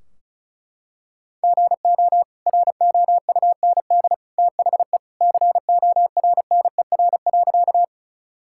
Morse code, heard as a keyed tone, about 34 wpm, 700 Hz. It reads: GO ROUND THE CORNER.
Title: Sentences from Top 1600 Words 35wpm